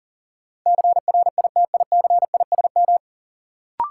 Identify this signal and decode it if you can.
Morse code, heard as a keyed tone, about 39 wpm, 700 Hz. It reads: CRITICISM
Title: Sentences from Top 1300 Words - New Word Drill Alt - 40wpm